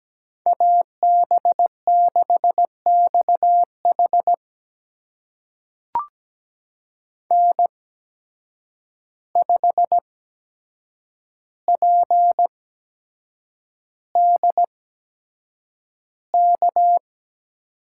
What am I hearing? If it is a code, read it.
Morse code, heard as a keyed tone, about 17 wpm, spaced as for 7 wpm, 700 Hz. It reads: AB6XH N5PDK
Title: USA Call Signs - 5 Characters - 8x Character Spacing - 17wpm